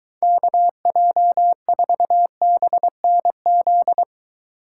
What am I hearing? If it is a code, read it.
Morse code, heard as a keyed tone, about 23 wpm, 700 Hz. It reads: KJ4BNZ